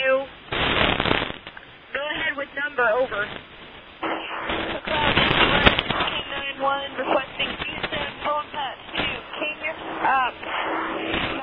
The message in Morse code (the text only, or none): none